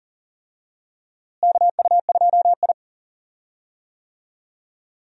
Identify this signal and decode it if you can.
Morse code, heard as a keyed tone, about 40 wpm, 700 Hz. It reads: KU2I
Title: USA Call Signs - 4 Characters - Repeated 2x - 40wpm